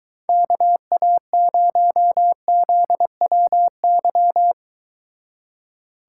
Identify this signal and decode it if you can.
Morse code, heard as a keyed tone, about 23 wpm, 700 Hz. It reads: KA0ZWY